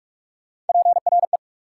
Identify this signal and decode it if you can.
Morse code, heard as a keyed tone, about 46 wpm, 690 Hz. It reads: PRE
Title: Prefixes and Suffixes 45wpm